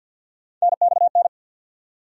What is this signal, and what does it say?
Morse code, heard as a keyed tone, about 48 wpm, 700 Hz. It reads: NXN
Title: USA Call Sign Postfixes - 3 Letters 50wpm